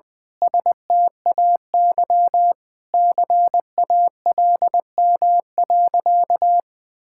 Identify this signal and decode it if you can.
Morse code, heard as a keyed tone, about 21 wpm, 700 Hz. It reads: STAY CALM.
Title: Sentences with 7-Word Phrases - 20wpm